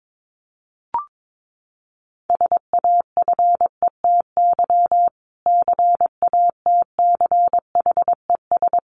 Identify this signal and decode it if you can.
Morse code, heard as a keyed tone, about 22 wpm, 700 Hz. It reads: SAFETY CATCHES
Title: A Sound Of Thunder 22wpm